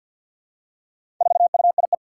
Morse code, heard as a keyed tone, about 49 wpm, 700 Hz. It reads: 4UIE
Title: Four Challenging Letter-Number Combinations 50wpm